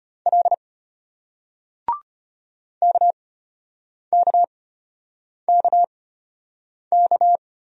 Morse code, heard as a keyed tone, about 34 wpm, spaced as for 8 wpm, 700 Hz. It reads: L KKKK